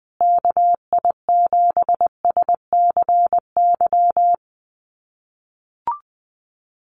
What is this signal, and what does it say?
Morse code, heard as a keyed tone, about 20 wpm, 700 Hz. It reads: KI7SCY
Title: USA Call Signs Repeated 3x 20wpm